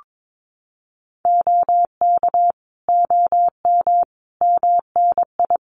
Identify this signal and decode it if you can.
Morse code, heard as a keyed tone, about 22 wpm, 700 Hz. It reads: OK OM MNI
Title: Common QSO Phrases 22wpm